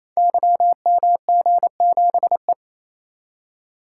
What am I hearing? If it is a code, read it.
Morse code, heard as a keyed tone, about 28 wpm, 700 Hz. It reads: YMG7E
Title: Code Groups - 5 Letter:Numbers - 28wpm